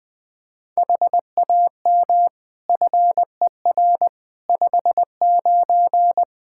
Morse code, heard as a keyed tone, about 20 wpm, 700 Hz. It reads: HAM FER 59
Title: Practice QSO - Exchange 3 - More Info - 20wpm